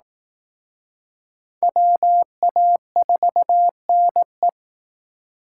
Morse code, heard as a keyed tone, about 19 wpm, 700 Hz. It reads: WA4NE
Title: Speed Racing - USA Call Signs 15wpm